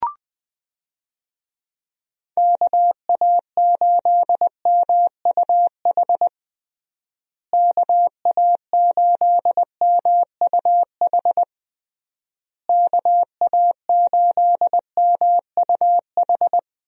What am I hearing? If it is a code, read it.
Morse code, heard as a keyed tone, about 21 wpm, 700 Hz. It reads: KA8MUH KA8MUH KA8MUH
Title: USA Call Signs - 6 Characters - Repeated 3x - 20wpm